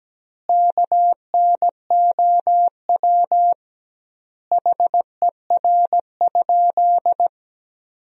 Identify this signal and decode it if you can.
Morse code, heard as a keyed tone, about 17 wpm, 700 Hz. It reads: KNOW HER?
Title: 2x Word-Spacing - Sentences from Top 100 Words - 17wpm